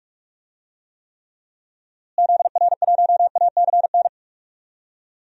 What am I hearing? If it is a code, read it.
Morse code, heard as a keyed tone, about 46 wpm, 700 Hz. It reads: ZR1ACN
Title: DXCC-100 Call Signs - 6 Characters - Repeated 3x - 45wpm